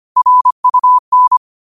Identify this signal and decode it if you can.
Morse code, heard as a keyed tone, about 25 wpm, 1000 Hz. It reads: RUN
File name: qotdfiles / 25wpm